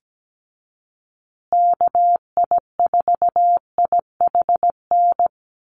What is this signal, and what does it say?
Morse code, heard as a keyed tone, about 17 wpm, 700 Hz. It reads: KI4IHN
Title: USA Call Signs Repeated 3x 17wpm